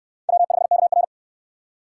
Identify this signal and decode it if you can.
Morse code, heard as a keyed tone, about 69 wpm, 700 Hz. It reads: W5KR